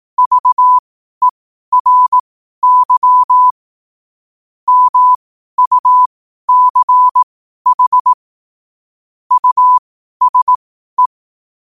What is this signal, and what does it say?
Morse code, heard as a keyed tone, about 18 wpm, spaced as for 12 wpm, 1000 Hz. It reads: VERY MUCH USE
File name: qotdfiles / 12wpm